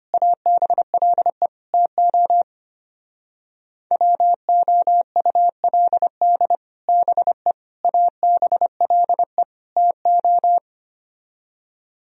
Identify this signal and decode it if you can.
Morse code, heard as a keyed tone, about 27 wpm, 700 Hz. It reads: ABLE TO WOULD BE ABLE TO
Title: Speed-Racing - Sets of 4 Words 25wpm